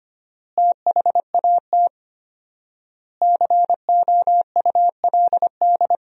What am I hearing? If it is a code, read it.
Morse code, heard as a keyed tone, about 25 wpm, 700 Hz. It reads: THAT COULD
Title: 4x Word-Spacing - Sentences from Top 400 Words - 25wpm